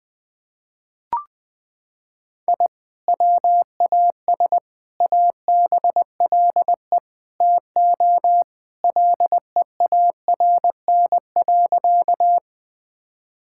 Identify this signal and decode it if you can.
Morse code, heard as a keyed tone, about 21 wpm, 700 Hz. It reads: I WAS ABLE TO LEARN.
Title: Sentences with 4-Word Phrases - 20wpm